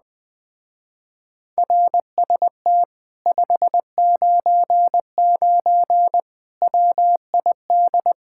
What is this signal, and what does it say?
Morse code, heard as a keyed tone, about 20 wpm, 700 Hz. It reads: RST 599 WID
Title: Speed-Racing Common QSO Phrases - 17wpm